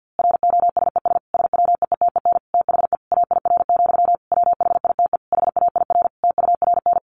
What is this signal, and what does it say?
Morse code, heard as a keyed tone, about 50 wpm, 700 Hz. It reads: ROSES SWEETEN THE AIR, WHITE HAIR TURN
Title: A Sound Of Thunder 50wpm